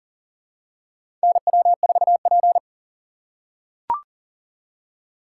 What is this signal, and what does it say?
Morse code, heard as a keyed tone, about 40 wpm, 700 Hz. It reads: NW4P
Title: USA Call Signs - 4 Characters - 6x Character Spacing - 40wpm